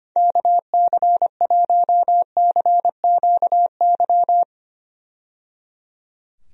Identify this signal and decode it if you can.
Morse code, heard as a keyed tone, about 25 wpm, 700 Hz. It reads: KC1CQY
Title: USA Call Signs - 6 Characters - 25wpm